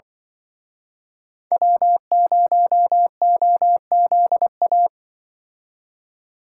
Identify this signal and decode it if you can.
Morse code, heard as a keyed tone, about 23 wpm, 700 Hz. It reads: W0OZA
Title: Speed Racing - USA 1x3 Call Signs 20wpm